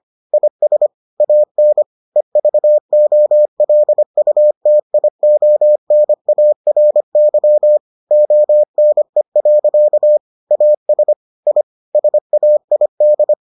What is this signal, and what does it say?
Morse code, heard as a keyed tone, about 25 wpm, 600 Hz. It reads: IS AN EVOLUTIONARY ONE. AS I SAID